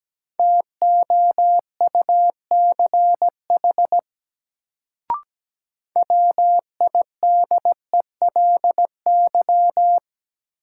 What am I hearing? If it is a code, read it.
Morse code, heard as a keyed tone, about 17 wpm, 700 Hz. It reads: TOUCH WIDELY
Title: Sentences from Top 1200 Words - New Word Drill - 17wpm